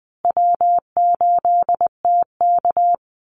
Morse code, heard as a keyed tone, about 20 wpm, 700 Hz. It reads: W8TK
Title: Easy Call Signs Repeated 3x - 20wpm